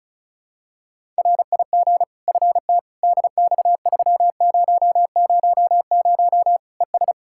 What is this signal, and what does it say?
Morse code, heard as a keyed tone, about 36 wpm, 700 Hz. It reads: RIG FTDX3000 ES
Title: Practice QSO - Exchange 2 - Info - 35wpm